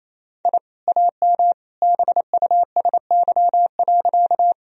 Morse code, heard as a keyed tone, about 28 wpm, 690 Hz. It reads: I AM BUSY.
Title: Sentences from Top 1000 Words - Review Part 3 - 28wpm